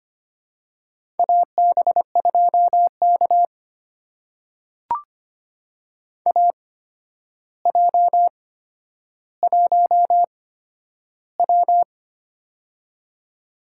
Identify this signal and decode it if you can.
Morse code, heard as a keyed tone, about 25 wpm, spaced as for 11 wpm, 700 Hz. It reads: AB2K AJ1W
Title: USA Call Signs - 4 Characters - 8x Character Spacing - 25wpm